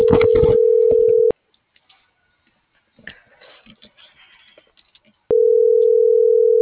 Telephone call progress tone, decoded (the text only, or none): ringback tone